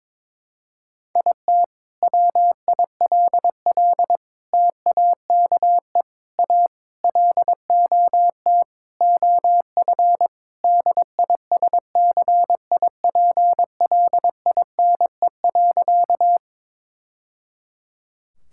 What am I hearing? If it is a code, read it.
Morse code, heard as a keyed tone, about 22 wpm, 700 Hz. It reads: IT WILL TAKE A LOT OF DISCIPLINE.
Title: Sentences from Top 1000 Words - Review Part 6 - 22wpm